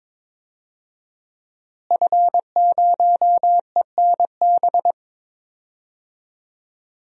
Morse code, heard as a keyed tone, about 21 wpm, 700 Hz. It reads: F0ENB
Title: DXCC-100 Call Signs - 5 Characters - Repeated 3x - 22wpm